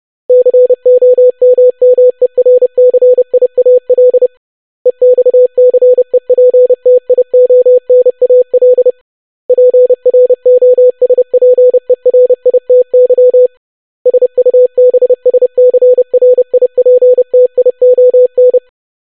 Morse code, heard as a keyed tone, about 30 wpm, 500 Hz. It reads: COMMERCIAL EXCEPTIONAL PROSPERITY SUBSCRIPTION
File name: longwords-25wpm